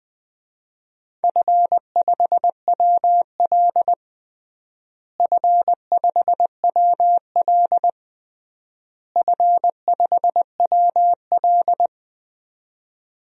Morse code, heard as a keyed tone, about 20 wpm, 700 Hz. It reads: F5WL F5WL F5WL
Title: DXCC-100 Call Signs - 4 Characters - Repeated 3x - 20wpm